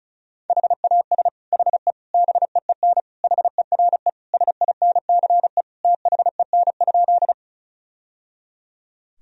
Morse code, heard as a keyed tone, about 36 wpm, 700 Hz. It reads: HAS HE BEEN HERE SINCE THEN?
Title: Sentences from Top 1000 Words - Review Part 1 - 35wpm